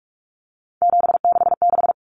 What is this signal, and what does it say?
Morse code, heard as a keyed tone, about 44 wpm, 700 Hz. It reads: 766
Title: Common Cut Numbers - 3-Digit 45wpm